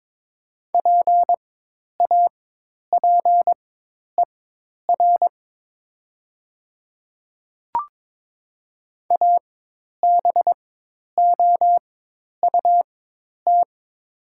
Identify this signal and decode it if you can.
Morse code, heard as a keyed tone, about 22 wpm, spaced as for 10 wpm, 700 Hz. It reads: PAPER ABOUT
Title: 4x Character Spacing - Top 300 Words - 22wpm